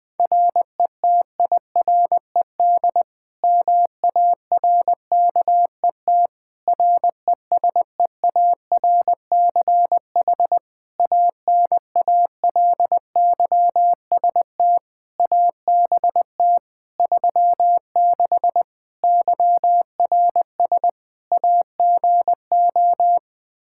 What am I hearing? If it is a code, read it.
Morse code, heard as a keyed tone, about 20 wpm, 700 Hz. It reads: RETIRED MARKET RESEARCH ANALYST ABT 36 YRS AGO
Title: Practice QSO - Exchange 3 - More Info - 20wpm